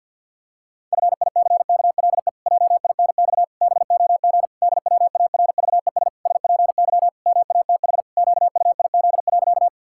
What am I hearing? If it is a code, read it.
Morse code, heard as a keyed tone, about 50 wpm, 700 Hz. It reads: FICKLE JINX BOG DWARVES SPY MATH QUIZ.
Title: Pangram Challenge v2 - 50wpm